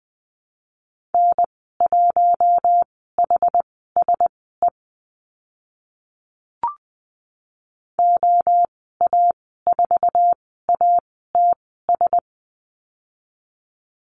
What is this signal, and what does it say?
Morse code, heard as a keyed tone, about 20 wpm, spaced as for 14 wpm, 700 Hz. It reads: N1HSE OA4ATS
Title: 082 Lesson 14g - Number 2 - Call Signs 20wpm